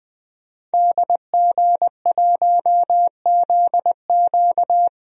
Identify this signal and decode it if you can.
Morse code, heard as a keyed tone, about 20 wpm, 700 Hz. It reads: DG1ZQ
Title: DXCC-100 Call Signs - 5 Characters - Repeated 2x - 20wpm